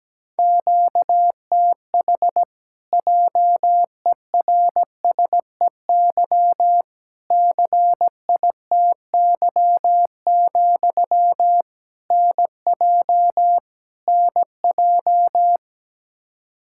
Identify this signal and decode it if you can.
Morse code, heard as a keyed tone, about 17 wpm, 700 Hz. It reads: QTH JERSEY CITY, NJ NJ
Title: Practice QSO - Exchange 1 - Beginning - 17wpm